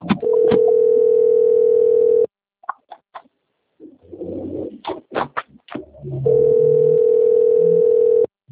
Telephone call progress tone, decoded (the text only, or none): ringback tone